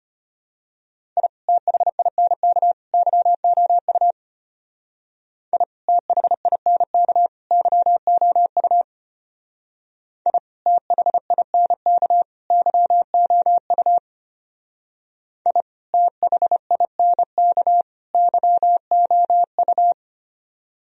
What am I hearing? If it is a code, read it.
Morse code, heard as a keyed tone, about 32 wpm, 700 Hz. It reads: I THINK YOU I THINK YOU I THINK YOU I THINK YOU